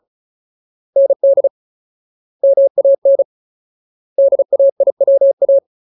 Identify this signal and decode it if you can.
Morse code, heard as a keyed tone, about 34 wpm, 550 Hz. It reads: ND MAN DAIWA
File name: Ultimate Word List / 35WPM Ultimate